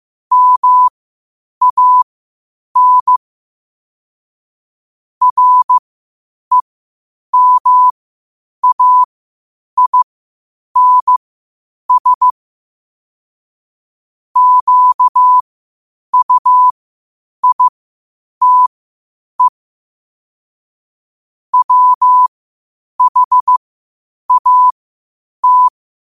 Morse code, heard as a keyed tone, about 15 wpm, spaced as for 8 wpm, 1000 Hz. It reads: MAN REMAINS QUITE WHAT
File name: qotdfiles / 8wpm